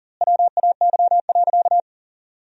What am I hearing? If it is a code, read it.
Morse code, heard as a keyed tone, about 40 wpm, 700 Hz. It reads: WAY.
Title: 2x Word-Spacing - Sentences from Top 500 Words - 40wpm